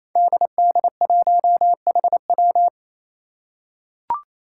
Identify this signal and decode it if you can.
Morse code, heard as a keyed tone, about 28 wpm, 700 Hz. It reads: DD1HW
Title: Five Challenging Letter-Number Combinations 28wpm